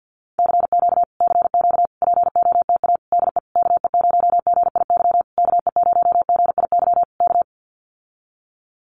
Morse code, heard as a keyed tone, about 50 wpm, 700 Hz. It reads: CQ CQ POTA DE KE0GIY KE0GIY K
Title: Common QSO Phrases Encore 50wpm